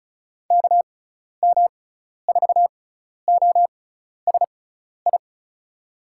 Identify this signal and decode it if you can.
Morse code, heard as a keyed tone, about 34 wpm, spaced as for 12 wpm, 700 Hz. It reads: KM4OSI
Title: USA Call Signs - 6x Character Spacing - 35wpm